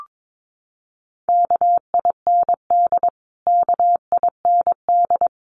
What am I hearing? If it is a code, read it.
Morse code, heard as a keyed tone, about 22 wpm, 700 Hz. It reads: KIND KIND
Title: Top 300 Words Repeated 22wpm